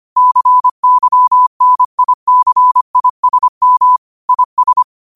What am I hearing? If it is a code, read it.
Morse code, heard as a keyed tone, about 25 wpm, 1000 Hz. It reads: CYNICISM IS
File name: qotdfiles / 25wpm